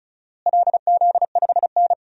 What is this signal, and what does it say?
Morse code, heard as a keyed tone, about 36 wpm, 700 Hz. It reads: LZ5N
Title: DXCC-100 Call Signs - 4 Characters - Repeated 2x - 35wpm